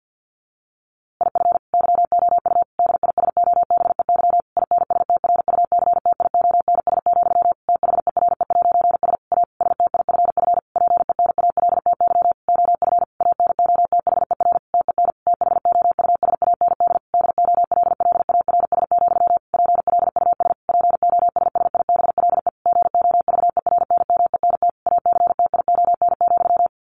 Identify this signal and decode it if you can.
Morse code, heard as a keyed tone, about 50 wpm, 700 Hz. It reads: IF YOU DISOBEY INSTRUCTIONS, THERE'S A STIFF PENALTY OF ANOTHER TEN THOUSAND DOLLARS, PLUS POSSIBLE GOVERNMENT ACTION,
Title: A Sound Of Thunder 50wpm